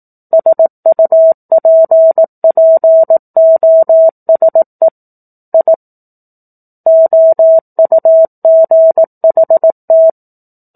Morse code, heard as a keyed tone, about 18 wpm, 650 Hz. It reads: SUPPOSE I OUGHT
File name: 18 WPM